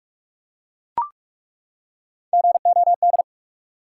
Morse code, heard as a keyed tone, about 43 wpm, 700 Hz. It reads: GOD
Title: Three Letter Words - 45wpm